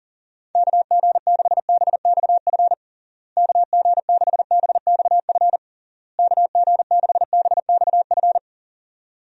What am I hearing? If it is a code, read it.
Morse code, heard as a keyed tone, about 40 wpm, 700 Hz. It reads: KG6BXF KG6BXF KG6BXF
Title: USA Call Signs - 6 Characters - Repeated 3x - 40wpm